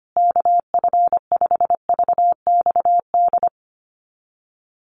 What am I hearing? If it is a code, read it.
Morse code, heard as a keyed tone, about 25 wpm, 700 Hz. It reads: KF5VXD
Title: USA Call Signs Repeated 3x 25wpm